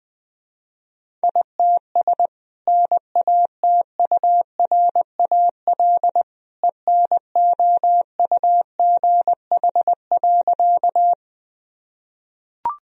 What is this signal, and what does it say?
Morse code, heard as a keyed tone, about 20 wpm, 700 Hz. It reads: ITS NATURAL ENOUGH.
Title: Sentences from Top 1000 Words - Review Part 4 - 20wpm